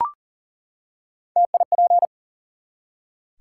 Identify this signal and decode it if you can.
Morse code, heard as a keyed tone, about 38 wpm, 700 Hz. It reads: TIP
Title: Three Letter Words - 40wpm